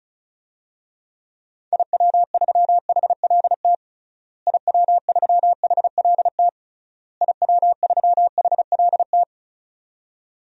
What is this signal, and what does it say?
Morse code, heard as a keyed tone, about 36 wpm, 700 Hz. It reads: IW3HLT IW3HLT IW3HLT
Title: DXCC-100 Call Signs - 6 Characters - Repeated 3x - 35wpm